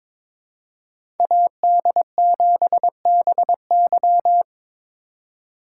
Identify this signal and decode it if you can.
Morse code, heard as a keyed tone, about 22 wpm, 700 Hz. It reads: AD7BY
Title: USA Call Signs - 3x Character Spacing - 22wpm